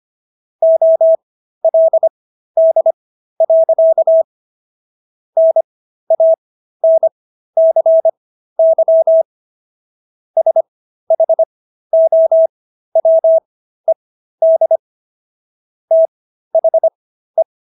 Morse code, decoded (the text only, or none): OLD. NANCY SHOWED THE